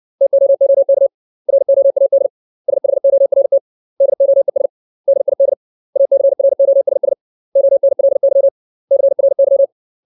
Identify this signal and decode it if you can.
Morse code, heard as a keyed tone, about 60 wpm, 550 Hz. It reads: TOOK ROAD SHORT DOES DID ACROSS ONLY CRY
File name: 632 words / 632 Words 60WPM